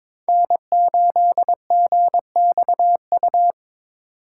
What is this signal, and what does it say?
Morse code, heard as a keyed tone, about 26 wpm, 700 Hz. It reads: N8GXU
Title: Speed Racing - USA 1x3 Call Signs 22wpm